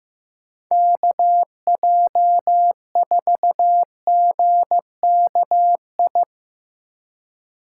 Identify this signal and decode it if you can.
Morse code, heard as a keyed tone, about 15 wpm, 700 Hz. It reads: KJ4GKI
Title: USA Call Signs - 6 Characters - 2x Character Spacing - 15wpm